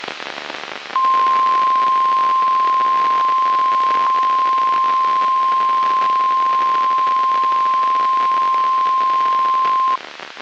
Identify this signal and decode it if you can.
Emergency Alert System (SAME) tones (the 1050 Hz attention tone), with noise. It is heard from 0.9 to 10.0 s.
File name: eas messages / EAS WXM95 Towanda